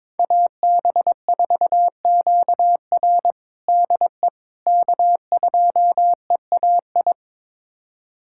Morse code, heard as a keyed tone, about 22 wpm, 700 Hz. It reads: AB4QR DE K2EAI
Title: USA Callsign DE Callsign - 5 Characters - 22wpm